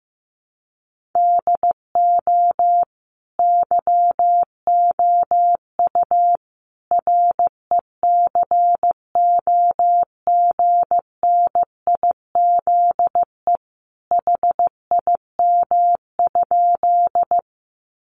Morse code, heard as a keyed tone, about 14 wpm, 700 Hz. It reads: DO YOU RECOGNIZE HIM?
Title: Sentences from Top 1400 Words 15wpm